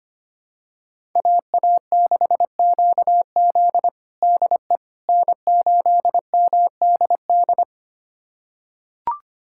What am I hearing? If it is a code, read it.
Morse code, heard as a keyed tone, about 25 wpm, 700 Hz. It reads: AA6QZ DE N8MDD
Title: USA Callsign DE Callsign - 5 Characters - 25wpm